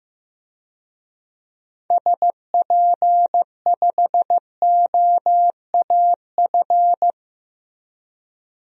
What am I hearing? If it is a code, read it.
Morse code, heard as a keyed tone, about 15 wpm, 700 Hz. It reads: SP5OAF
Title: DXCC-100 Call Signs - 6 Characters - 15wpm